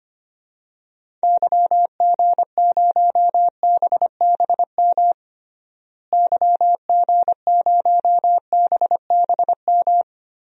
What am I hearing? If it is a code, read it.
Morse code, heard as a keyed tone, about 25 wpm, 700 Hz. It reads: YG0BBM YG0BBM
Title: DXCC-100 Call Signs - 6 Characters - Repeated 3x - 25wpm